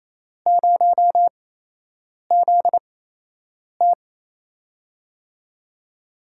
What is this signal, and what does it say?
Morse code, heard as a keyed tone, about 28 wpm, spaced as for 8 wpm, 700 Hz. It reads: 0ZT
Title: USA Call Signs - 5 Characters - 8x Character Spacing - 28wpm